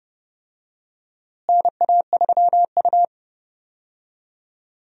Morse code, heard as a keyed tone, about 32 wpm, 700 Hz. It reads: NA3U
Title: USA Call Signs - 4 Characters - Repeated 3x - 30wpm